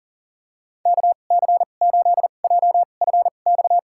Morse code, heard as a keyed tone, about 40 wpm, spaced as for 29 wpm, 700 Hz. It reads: KC8JFX
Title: USA Call Signs - 6 Characters - 2x Character Spacing - 40wpm